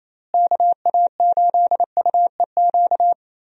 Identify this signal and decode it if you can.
Morse code, heard as a keyed tone, about 28 wpm, 700 Hz. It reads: KA8UEQ
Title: USA Call Signs - 6 Characters - Repeated 3x - 28wpm